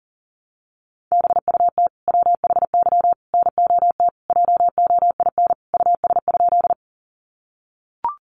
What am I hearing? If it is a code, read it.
Morse code, heard as a keyed tone, about 38 wpm, 700 Hz. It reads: BUT WHY NOT JOIN US?
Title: Sentences from Top 1400 Words 40wpm